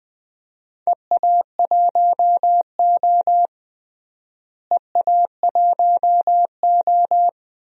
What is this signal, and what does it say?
Morse code, heard as a keyed tone, about 20 wpm, 700 Hz. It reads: EA1O EA1O
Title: DXCC-100 Call Signs - 4 Characters - Repeated 2x - 20wpm